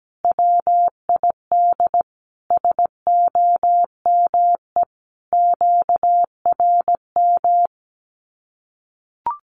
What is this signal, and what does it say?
Morse code, heard as a keyed tone, about 17 wpm, 700 Hz. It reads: WID SOME QRM
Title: Common QSO Phrases 17wpm